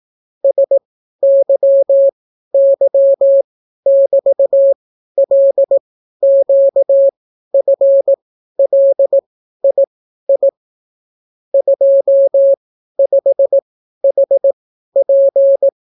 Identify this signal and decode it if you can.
Morse code, heard as a keyed tone, about 18 wpm, spaced as for 12 wpm, 550 Hz. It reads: SYY<BT>LQFLII 25HP